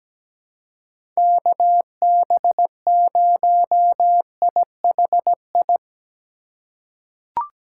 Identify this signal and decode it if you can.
Morse code, heard as a keyed tone, about 17 wpm, 700 Hz. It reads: KB0IHI
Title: USA Call Signs - 6 Characters - 4x Character Spacing - 17wpm